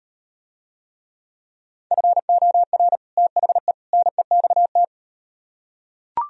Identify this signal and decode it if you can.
Morse code, heard as a keyed tone, about 38 wpm, 700 Hz. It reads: FOR THE NEXT